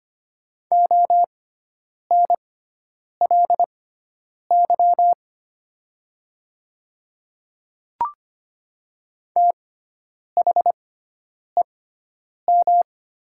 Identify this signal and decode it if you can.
Morse code, heard as a keyed tone, about 25 wpm, spaced as for 9 wpm, 700 Hz. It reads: ONLY THEM
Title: 6x Character Spacing - Top 200 Words - 25wpm